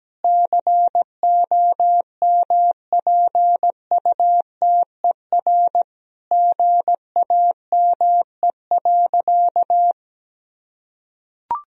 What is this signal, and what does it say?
Morse code, heard as a keyed tone, about 17 wpm, 700 Hz. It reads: COMPUTER GAME.
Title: Sentences from Top 1000 Words - Review Part 3 - 17wpm